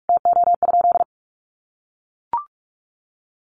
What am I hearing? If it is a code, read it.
Morse code, heard as a keyed tone, about 44 wpm, 700 Hz. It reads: TO?
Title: Sentences from Top 100 Words 45wpm